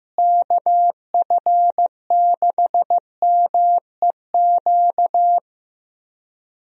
Morse code, heard as a keyed tone, about 15 wpm, 700 Hz. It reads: KF6MEQ
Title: USA Call Signs - 6 Characters - Repeated 2x - 15wpm